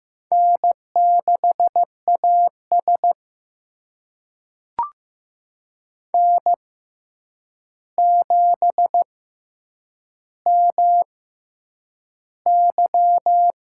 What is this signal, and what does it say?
Morse code, heard as a keyed tone, about 15 wpm, spaced as for 8 wpm, 700 Hz. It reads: N6AS N7MY
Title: USA Call Signs - 4 Characters - 6x Character Spacing - 15wpm